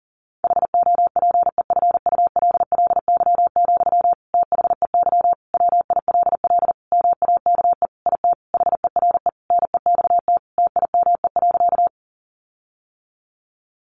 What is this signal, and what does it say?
Morse code, heard as a keyed tone, about 40 wpm, 700 Hz. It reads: HOPEFULLY, THEY WILL MAKE IT HERE NEXT TIME.
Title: Sentences from Top 1500 Words 40wpm